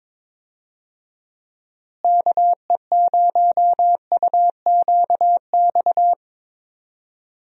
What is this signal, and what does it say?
Morse code, heard as a keyed tone, about 22 wpm, 700 Hz. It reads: KE0UQX
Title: USA Call Signs - 6 Characters - Repeated 3x - 22wpm